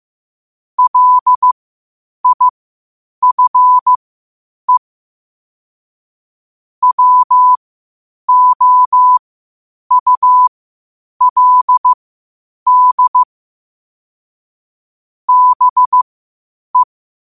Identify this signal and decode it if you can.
Morse code, heard as a keyed tone, about 15 wpm, spaced as for 8 wpm, 1000 Hz. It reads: LIFE WOULD BE
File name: qotdfiles / 8wpm